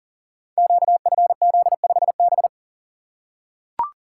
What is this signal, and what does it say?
Morse code, heard as a keyed tone, about 41 wpm, 700 Hz. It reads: QFZ5B
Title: Code Groups - 5 Letter:Numbers - 40wpm